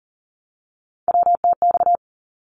Morse code, heard as a keyed tone, about 40 wpm, 700 Hz. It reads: WTX
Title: ARRL Sections 40wpm